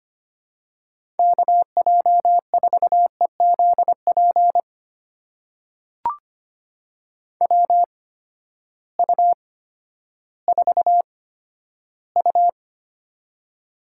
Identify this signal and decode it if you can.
Morse code, heard as a keyed tone, about 25 wpm, spaced as for 13 wpm, 700 Hz. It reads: KJ4EZP WU4U
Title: USA Call Signs - 8x Character Spacing - 25wpm